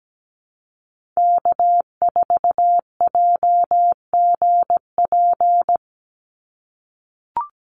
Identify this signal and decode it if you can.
Morse code, heard as a keyed tone, about 17 wpm, 700 Hz. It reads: K4JGP
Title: USA Call Signs 17wpm